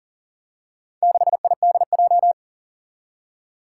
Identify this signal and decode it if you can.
Morse code, heard as a keyed tone, about 40 wpm, 700 Hz. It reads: 6IDJ
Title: Four Challenging Letter-Number Combinations 40wpm